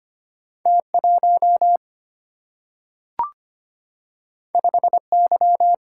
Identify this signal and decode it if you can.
Morse code, heard as a keyed tone, about 25 wpm, 700 Hz. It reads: T 1 5 Y
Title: Single-Letter Single-Number 25wpm